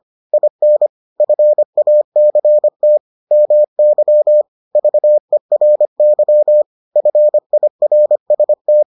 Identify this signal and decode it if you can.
Morse code, heard as a keyed tone, about 25 wpm, 600 Hz. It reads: IN FACT MY VERY FIRST